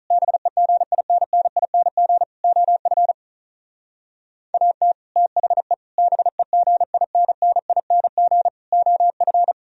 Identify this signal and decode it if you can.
Morse code, heard as a keyed tone, about 43 wpm, 700 Hz. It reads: BEGINNING OF AT THE BEGINNING OF
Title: Speed-Racing - Sets of 4 Words 35wpm